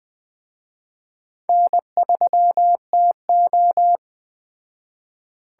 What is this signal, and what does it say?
Morse code, heard as a keyed tone, about 20 wpm, 700 Hz. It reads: N3TO
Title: Speed Racing - USA Call Signs 17wpm